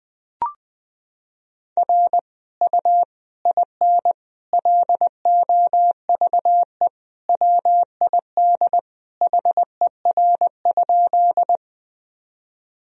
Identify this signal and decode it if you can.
Morse code, heard as a keyed tone, about 20 wpm, 700 Hz. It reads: R U IN LOVE WID HER?
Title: Sentences from Top 400 Words and Abbreviations 20wpm